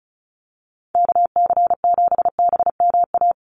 Morse code, heard as a keyed tone, about 34 wpm, 700 Hz. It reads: KC7BMA
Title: USA Call Signs 35wpm